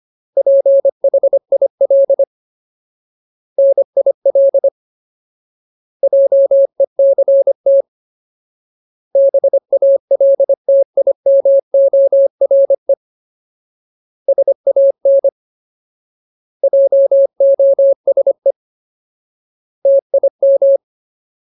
Morse code, heard as a keyed tone, about 25 wpm, 550 Hz. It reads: PHIL NIL JECT BALTIMORE SAN JOSE TIM
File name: Ultimate Word List / 25WPM Ultimate